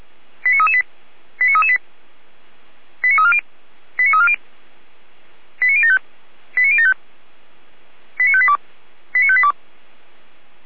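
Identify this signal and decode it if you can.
Five-tone selective call (ZVEI), heard as quick runs of five tones, tones of about 80 ms. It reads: 89338 89338 89340 89340 89075 89075 89582 89582
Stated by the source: Five-tone ZVEI selective call received on SDR.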